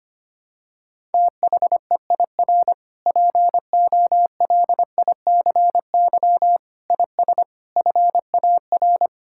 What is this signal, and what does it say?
Morse code, heard as a keyed tone, about 25 wpm, 700 Hz. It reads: THEIR POLICY IS FAR
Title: Sentences from Top 1000 Words - Review Part 5 - 25wpm